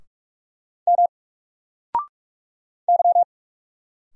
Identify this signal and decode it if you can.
Morse code, heard as a keyed tone, about 44 wpm, 700 Hz. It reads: M Y
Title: Single Letter-Number Alt 45wpm